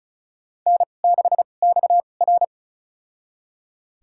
Morse code, heard as a keyed tone, about 36 wpm, spaced as for 25 wpm, 700 Hz. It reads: N6XR
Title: USA Call Signs - 2x Character Spacing - 35wpm